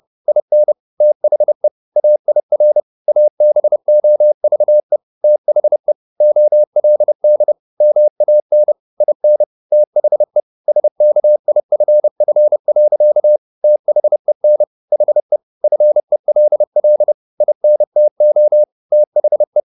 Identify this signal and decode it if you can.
Morse code, heard as a keyed tone, about 30 wpm, 600 Hz. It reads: IN THE AIR ABOVE THE OLD MAN IN THE SKIFF. THEN HE FELL INTO THE